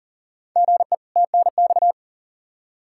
Morse code, heard as a keyed tone, about 40 wpm, 700 Hz. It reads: GE TNX
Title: Practice QSO - Exchange 1 - Beginning - 40wpm